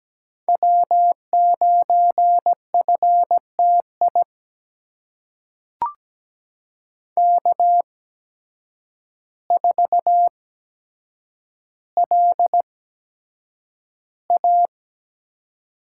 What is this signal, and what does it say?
Morse code, heard as a keyed tone, about 17 wpm, spaced as for 8 wpm, 700 Hz. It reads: W9FTI K4LA
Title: USA Call Signs - 5 Characters - 8x Character Spacing - 17wpm